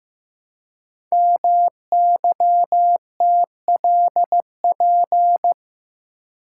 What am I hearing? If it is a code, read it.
Morse code, heard as a keyed tone, about 15 wpm, 700 Hz. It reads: MYTLP
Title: Code Groups - 5 Letters - 15wpm